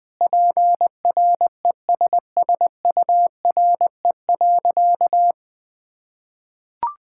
Morse code, heard as a keyed tone, about 21 wpm, 700 Hz. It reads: PRESSURE.
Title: Sentences from Top 1000 Words - Review Part 6 - 20wpm